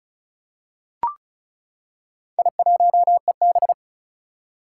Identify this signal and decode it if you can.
Morse code, heard as a keyed tone, about 36 wpm, 700 Hz. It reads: I1EB
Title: Four Challenging Letter-Number Combinations 35wpm